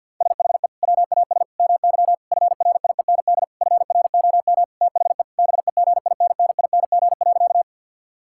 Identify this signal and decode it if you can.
Morse code, heard as a keyed tone, about 50 wpm, 700 Hz. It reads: SHE WAS MY FRIEND FROM THE BEGINNING.
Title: Sentences from Top 1000 Words - Review Part 5 - 50wpm